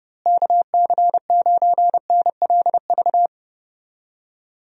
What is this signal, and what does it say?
Morse code, heard as a keyed tone, about 30 wpm, 700 Hz. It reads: KC9NLV
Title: USA Call Signs - 6 Characters - Repeated 3x - 30wpm